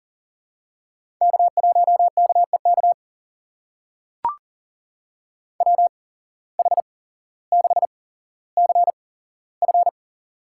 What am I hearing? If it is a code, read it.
Morse code, heard as a keyed tone, about 37 wpm, spaced as for 17 wpm, 700 Hz. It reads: K1KEK WH6CF
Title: USA Call Signs - 5 Characters - 8x Character Spacing - 40wpm